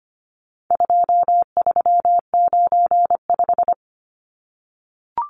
Morse code, heard as a keyed tone, about 25 wpm, 700 Hz. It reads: 2395
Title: Numbers - 4-Digit 25wpm